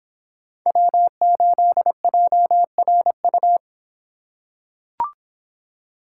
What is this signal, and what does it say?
Morse code, heard as a keyed tone, about 26 wpm, 700 Hz. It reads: W8JRU